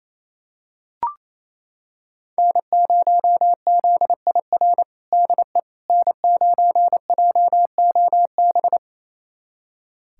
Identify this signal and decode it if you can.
Morse code, heard as a keyed tone, about 27 wpm, 700 Hz. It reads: N0ZIR DE N9JOB
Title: USA Callsign DE Callsign - 5 Characters - 28wpm